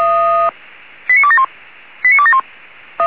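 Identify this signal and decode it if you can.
Five-tone selective call (ZVEI), heard as quick runs of five tones, tones of about 70 ms. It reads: 89271 89271
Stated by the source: Five-tone ZVEI selective call 89273 received by SDR.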